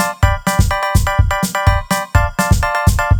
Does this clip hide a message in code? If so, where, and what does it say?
No.